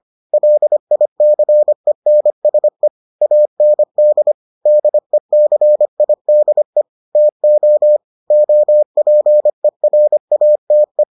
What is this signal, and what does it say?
Morse code, heard as a keyed tone, about 25 wpm, 600 Hz. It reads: LICENSE AND DECIDE TO OPERATE